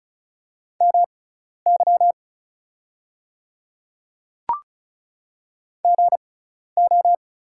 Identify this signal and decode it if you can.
Morse code, heard as a keyed tone, about 36 wpm, spaced as for 12 wpm, 690 Hz. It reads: MY GO
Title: 6x Character Spacing - Top 500 Words - 35wpm